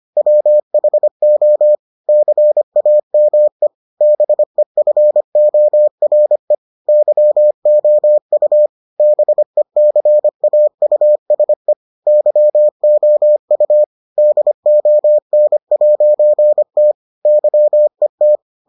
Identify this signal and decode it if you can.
Morse code, heard as a keyed tone, about 25 wpm, 600 Hz. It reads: WHO CAME BEFORE YOU BECAUSE YOU DON'T YET